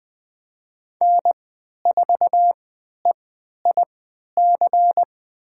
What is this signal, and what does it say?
Morse code, heard as a keyed tone, about 21 wpm, spaced as for 12 wpm, 700 Hz. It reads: N4EIC
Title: USA Call Signs - 5 Characters - 3x Character Spacing - 20wpm